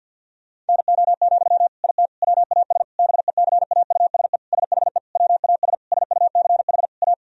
Morse code, heard as a keyed tone, about 50 wpm, 700 Hz. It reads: NO, IT WAS BECAUSE SHE WAS SUCH A
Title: Sentences from Top 1000 Words - Review Part 4 - 50wpm